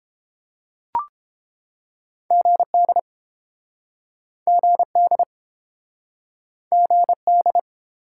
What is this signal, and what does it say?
Morse code, heard as a keyed tone, about 29 wpm, 700 Hz. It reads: GD GD GD